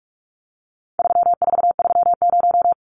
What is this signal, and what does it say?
Morse code, heard as a keyed tone, about 44 wpm, 700 Hz. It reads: 3430
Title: Numbers - 4-Digit 45wpm